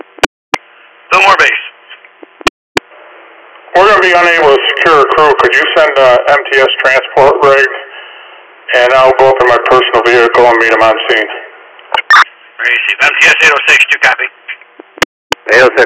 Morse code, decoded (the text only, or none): none